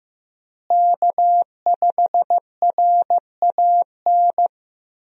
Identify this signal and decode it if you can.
Morse code, heard as a keyed tone, about 15 wpm, 700 Hz. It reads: K5RAN
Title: USA Call Signs - 5 Characters - Repeated 3x - 15wpm